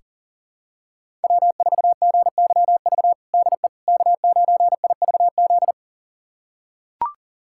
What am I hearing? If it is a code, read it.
Morse code, heard as a keyed tone, about 40 wpm, 700 Hz. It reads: W4GYV DE K9IVZ
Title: USA Callsign DE Callsign - 5 Characters - 40wpm